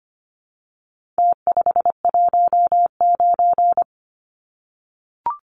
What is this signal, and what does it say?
Morse code, heard as a keyed tone, about 25 wpm, 700 Hz. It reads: T519
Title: Numbers - Contest Style - 4-Digit 25wpm